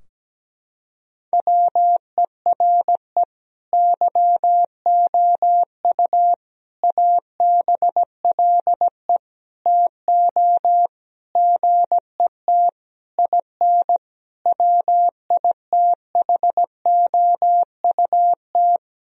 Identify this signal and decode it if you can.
Morse code, heard as a keyed tone, about 17 wpm, 700 Hz. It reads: WERE YOU ABLE TO GET IN WITHOUT
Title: Sentences from Top 1000 Words - Review Part 4 - 17wpm